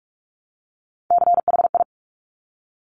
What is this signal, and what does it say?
Morse code, heard as a keyed tone, about 46 wpm, 700 Hz. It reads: CHI
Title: Top Three Letter Combinations in Words 45wpm